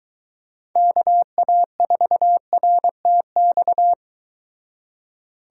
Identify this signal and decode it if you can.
Morse code, heard as a keyed tone, about 23 wpm, 700 Hz. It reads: KA4RTX